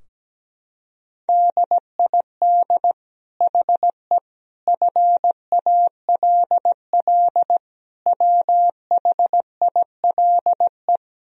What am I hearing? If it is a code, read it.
Morse code, heard as a keyed tone, about 17 wpm, 700 Hz. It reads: DID HE FALL WHILE
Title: Sentences from Top 1000 Words - Review Part 2 - 17wpm